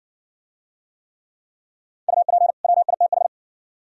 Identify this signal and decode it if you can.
Morse code, heard as a keyed tone, about 57 wpm, 690 Hz. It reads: UP WITH